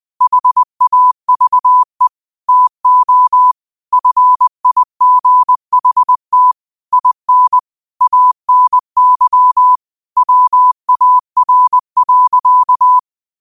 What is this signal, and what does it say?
Morse code, heard as a keyed tone, about 20 wpm, 1000 Hz. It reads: HAVE TO FIGHT IN ANY WAR.
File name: qotdfiles / 20wpm